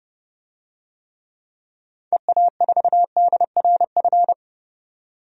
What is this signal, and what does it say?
Morse code, heard as a keyed tone, about 30 wpm, 700 Hz. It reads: EA4DRF
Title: DXCC-100 Call Signs - 6 Characters - 30wpm